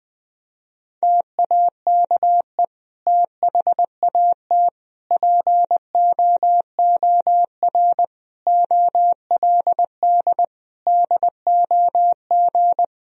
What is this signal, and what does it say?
Morse code, heard as a keyed tone, about 20 wpm, 700 Hz. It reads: TAKE THAT POOR OLD DOG
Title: Sentences from Top 1000 Words - Review Part 2 - 20wpm